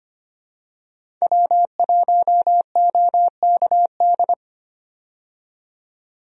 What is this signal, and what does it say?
Morse code, heard as a keyed tone, about 25 wpm, 700 Hz. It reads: W1OKD
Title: USA Call Signs - 5 Characters - 25wpm